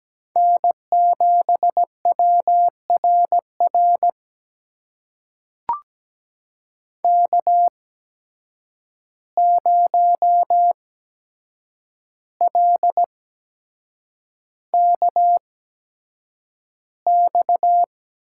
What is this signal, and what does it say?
Morse code, heard as a keyed tone, about 17 wpm, spaced as for 7 wpm, 700 Hz. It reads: N7WRR K0LKX
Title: USA Call Signs - 5 Characters - 8x Character Spacing - 17wpm